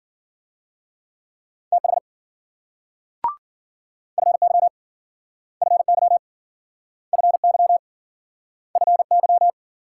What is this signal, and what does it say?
Morse code, heard as a keyed tone, about 53 wpm, 700 Hz. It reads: TH FY FY FY FY